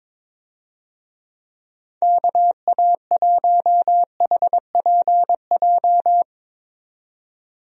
Morse code, heard as a keyed tone, about 22 wpm, 700 Hz. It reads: KA1HPJ
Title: USA Call Signs - 6 Characters - 22wpm